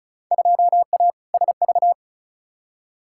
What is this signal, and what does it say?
Morse code, heard as a keyed tone, about 36 wpm, 700 Hz. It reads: 2A SV
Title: Field Day Exchanges - 35wpm